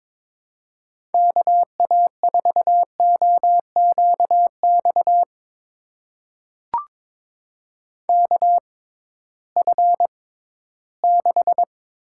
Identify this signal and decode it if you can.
Morse code, heard as a keyed tone, about 22 wpm, spaced as for 14 wpm, 700 Hz. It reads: KA4OQX KF6
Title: USA Call Signs - 6 Characters - 6x Character Spacing - 22wpm